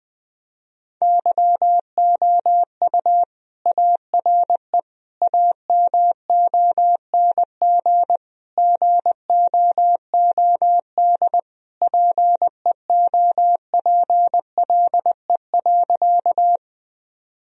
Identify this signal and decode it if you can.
Morse code, heard as a keyed tone, about 21 wpm, 700 Hz. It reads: YOU ARE AMONG GOOD PEOPLE.
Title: Sentences from Top 1000 Words - Review Part 3 - 20wpm